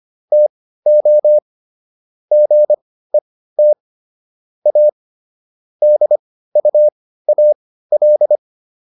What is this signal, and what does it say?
Morse code, heard as a keyed tone, about 25 wpm, spaced as for 15 wpm, 600 Hz. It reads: TO GET A DUAL